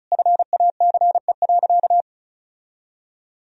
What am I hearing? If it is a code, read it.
Morse code, heard as a keyed tone, about 34 wpm, 700 Hz. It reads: FACE.
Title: Sentences from Top 1000 Words - Review Part 2 - 35wpm